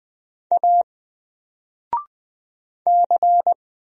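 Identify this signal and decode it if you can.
Morse code, heard as a keyed tone, about 20 wpm, 700 Hz. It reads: A C
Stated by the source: Single letter and number Morse drill at 20 wpm, from N4LCD.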